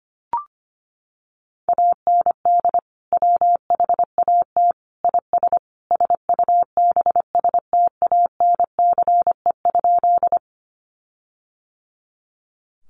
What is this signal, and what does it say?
Morse code, heard as a keyed tone, about 25 wpm, 700 Hz. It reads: AND WHAT IS SUBSTANCE?
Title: Sentences from Top 1500 Words 25wpm